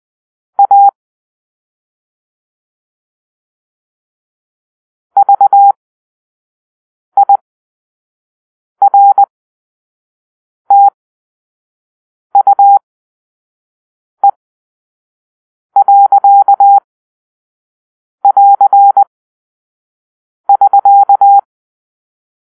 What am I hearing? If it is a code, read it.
Morse code, heard as a keyed tone, about 20 wpm, spaced as for 5 wpm, 800 Hz. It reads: A VIRTUE.<AR><SK>
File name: arrl / Grp-5 Drill-3